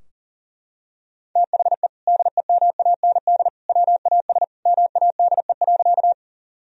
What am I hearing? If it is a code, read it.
Morse code, heard as a keyed tone, about 37 wpm, 700 Hz. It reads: THE DEMAND WAS MADE.
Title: Sentences from Top 1000 Words - Review Part 4 - 40wpm